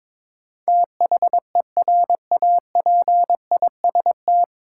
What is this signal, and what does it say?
Morse code, heard as a keyed tone, about 22 wpm, 700 Hz. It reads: THERAPIST
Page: Morse Code Ninja practice 22wpm